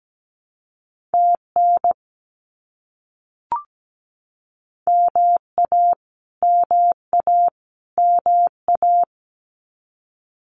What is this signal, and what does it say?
Morse code, heard as a keyed tone, about 17 wpm, 700 Hz. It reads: TN MA MA MA
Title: US States Repeated 3x 17wpm